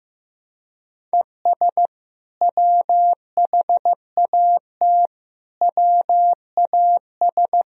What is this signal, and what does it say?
Morse code, heard as a keyed tone, about 15 wpm, 700 Hz. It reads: ES WHAT WAS
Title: Sentences from Top 200 Words and Abbreviations 15wpm